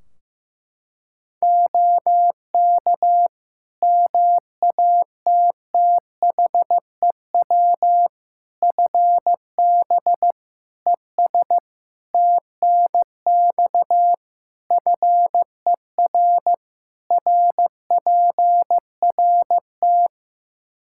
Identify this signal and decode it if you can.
Morse code, heard as a keyed tone, about 15 wpm, 700 Hz. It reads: OK MATTHEW FB ES TNX FER RPRT
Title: Practice QSO - Exchange 2 - Info - 15wpm